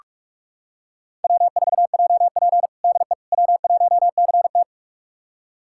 Morse code, heard as a keyed tone, about 46 wpm, 700 Hz. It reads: W4JP DE W1CT
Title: USA Callsign DE Callsign - 4 Characters - 45wpm